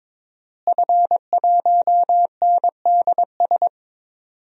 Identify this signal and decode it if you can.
Morse code, heard as a keyed tone, about 22 wpm, 700 Hz. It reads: F1NDS
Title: DXCC-100 Call Signs - 5 Characters - Repeated 3x - 22wpm